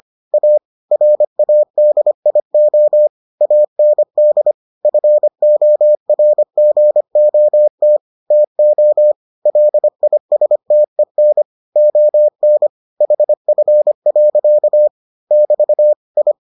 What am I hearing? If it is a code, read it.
Morse code, heard as a keyed tone, about 25 wpm, 600 Hz. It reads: A RADIO AND FORGOT TO LISTEN ON HF. <BT> I